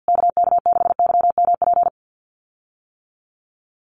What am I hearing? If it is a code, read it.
Morse code, heard as a keyed tone, about 50 wpm, 700 Hz. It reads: KK6YMP
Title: USA Call Signs Repeated 3x 50wpm